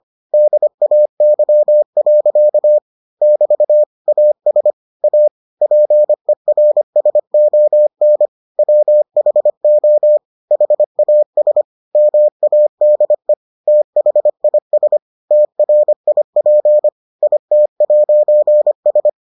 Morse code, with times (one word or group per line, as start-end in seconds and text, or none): 0.3-2.8 DAY.
3.2-3.8 <BT>
4.1-4.7 AS
5.0-5.3 A
5.6-8.3 PERSON
8.6-10.2 WHO
10.5-11.6 HAS
12.0-13.3 MADE
13.7-15.0 THIS
15.3-16.9 TRIP
17.2-19.1 IT'S